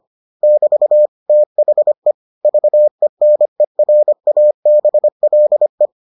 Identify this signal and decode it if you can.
Morse code, heard as a keyed tone, about 25 wpm, 600 Hz. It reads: <BT> THE VENERABLE